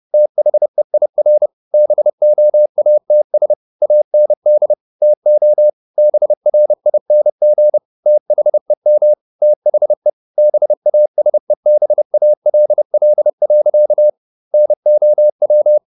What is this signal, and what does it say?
Morse code, heard as a keyed tone, about 30 wpm, 600 Hz. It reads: THEIR BOATS AND TO BRING THEM THE BASEBALL. NOW
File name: the old man and the sea / the old man and the sea 1 30wpm